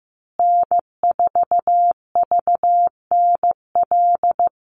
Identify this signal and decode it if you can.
Morse code, heard as a keyed tone, about 15 wpm, 700 Hz. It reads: N4VNL
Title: USA Call Signs Repeated 3x 15wpm